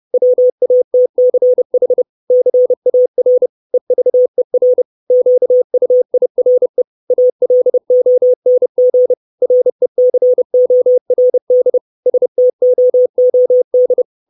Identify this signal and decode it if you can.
Morse code, heard as a keyed tone, about 30 wpm, 500 Hz. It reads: WATCH CAR EVER QUIRE ALONG RECORD STOOD
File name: 632 Words 30WPM